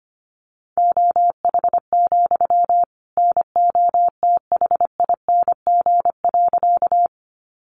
Morse code, heard as a keyed tone, about 25 wpm, 700 Hz. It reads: OH, NOTHING.
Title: Sentences from Top 500 Words 25wpm